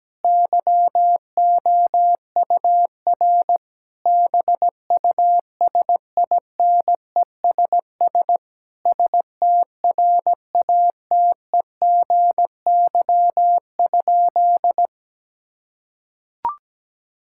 Morse code, heard as a keyed tone, about 17 wpm, 700 Hz. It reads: YOUR BUSINESS STRATEGY?
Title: Sentences from Top 1000 Words - Review Part 6 - 17wpm